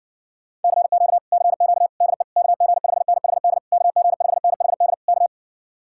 Morse code, heard as a keyed tone, about 61 wpm, 700 Hz. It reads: CQ CQ DE KG5NHD KG5NHD K